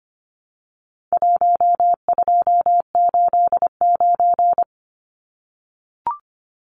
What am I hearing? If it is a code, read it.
Morse code, heard as a keyed tone, about 25 wpm, 700 Hz. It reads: 1289
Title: Numbers - Contest Style - 4-Digit 25wpm